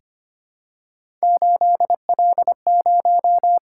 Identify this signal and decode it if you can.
Morse code, heard as a keyed tone, about 25 wpm, 700 Hz. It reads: 8L0
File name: Three Letter-Number Combinations - 25wpm